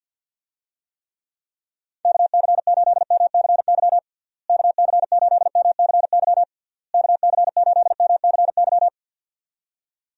Morse code, heard as a keyed tone, about 52 wpm, 700 Hz. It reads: KC8MCY KC8MCY KC8MCY
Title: USA Call Signs - 6 Characters - Repeated 3x - 50wpm